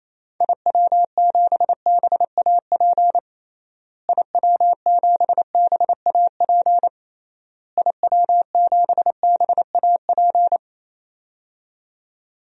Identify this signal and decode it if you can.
Morse code, heard as a keyed tone, about 28 wpm, 700 Hz. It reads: IW7BAP IW7BAP IW7BAP
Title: DXCC-100 Call Signs - 6 Characters - Repeated 3x - 28wpm